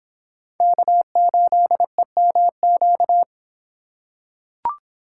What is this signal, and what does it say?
Morse code, heard as a keyed tone, about 24 wpm, 700 Hz. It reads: K8EMQ